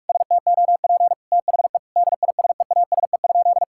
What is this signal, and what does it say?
Morse code, heard as a keyed tone, about 41 wpm, 700 Hz. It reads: STOP THE DISEASE?
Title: Sentences from Top 1000 Words - Review Part 6 - 45wpm